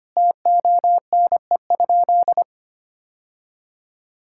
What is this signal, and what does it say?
Morse code, heard as a keyed tone, about 25 wpm, 700 Hz. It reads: TONE?
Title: Sentences from Top 1600 Words 25wpm